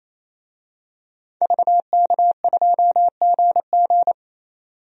Morse code, heard as a keyed tone, about 28 wpm, 700 Hz. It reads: VK2GG
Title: DXCC-100 Call Signs - 5 Characters - 28wpm